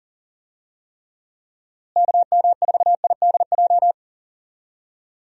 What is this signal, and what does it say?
Morse code, heard as a keyed tone, about 40 wpm, 700 Hz. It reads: KM4IDJ
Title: USA Call Signs - 6 Characters - Repeated 3x - 40wpm